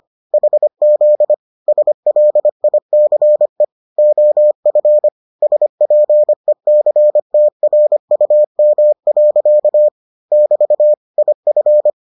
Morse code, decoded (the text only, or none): HZ SLICE OF SPECTRUM. <BT> IF